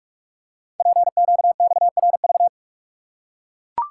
Morse code, heard as a keyed tone, about 44 wpm, 700 Hz. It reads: PQXRV
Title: Code Groups - 5 Letters - 45wpm